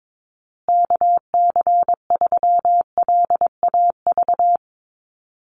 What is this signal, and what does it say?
Morse code, heard as a keyed tone, about 22 wpm, 690 Hz. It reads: KC3LAV
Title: USA Call Signs Repeated 3x 22wpm